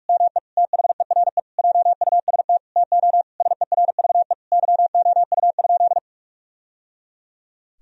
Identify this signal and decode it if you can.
Morse code, heard as a keyed tone, about 47 wpm, 700 Hz. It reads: ME THERE JUST TO SERVE YOU?
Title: Sentences from Top 1000 Words - Review Part 2 - 45wpm